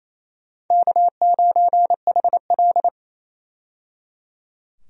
Morse code, heard as a keyed tone, about 27 wpm, 700 Hz. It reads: K9HL
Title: USA Call Signs - 4 Characters - 28wpm